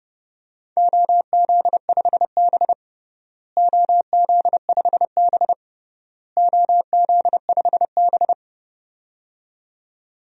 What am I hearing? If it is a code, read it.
Morse code, heard as a keyed tone, about 31 wpm, 700 Hz. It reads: OZ5B OZ5B OZ5B
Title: DXCC-100 Call Signs - 4 Characters - Repeated 3x - 30wpm